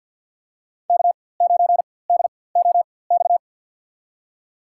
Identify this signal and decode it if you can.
Morse code, heard as a keyed tone, about 51 wpm, spaced as for 23 wpm, 700 Hz. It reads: K9DOX
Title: USA Call Signs - 5 Characters - 4x Character Spacing - 50wpm